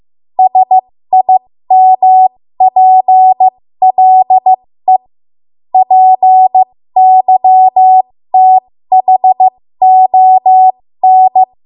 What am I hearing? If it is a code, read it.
Morse code, heard as a keyed tone, about 15 wpm, spaced as for 13 wpm, 750 Hz. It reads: SIMPLE PYTHON
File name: Morse / 13WPM